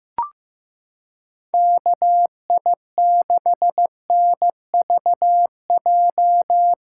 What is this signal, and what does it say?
Morse code, heard as a keyed tone, about 15 wpm, 700 Hz. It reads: KI6NVJ
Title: USA Callsign DE Callsign - 6 Characters - 15wpm